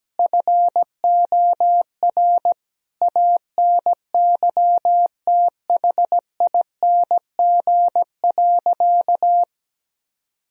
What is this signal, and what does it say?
Morse code, heard as a keyed tone, about 17 wpm, 700 Hz. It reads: FOR ANYTHING.
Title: Sentences from Top 1000 Words - Review Part 2 - 17wpm